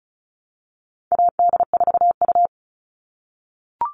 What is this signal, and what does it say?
Morse code, heard as a keyed tone, about 34 wpm, 700 Hz. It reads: AD4U
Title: All Cut Numbers - 4-Digit 35wpm